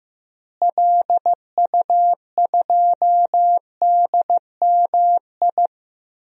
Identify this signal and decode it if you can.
Morse code, heard as a keyed tone, about 15 wpm, 700 Hz. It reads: LU2DMI
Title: DXCC-100 Call Signs - 6 Characters - Repeated 3x - 15wpm